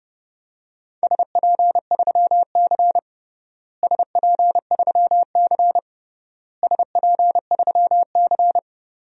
Morse code, heard as a keyed tone, about 32 wpm, 700 Hz. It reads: SP3C SP3C SP3C
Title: DXCC-100 Call Signs - 4 Characters - Repeated 3x - 30wpm